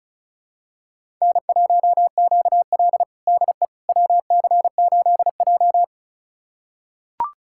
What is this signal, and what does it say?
Morse code, heard as a keyed tone, about 36 wpm, 700 Hz. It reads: N1QL DE WC8J
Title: USA Callsign DE Callsign - 4 Characters - 35wpm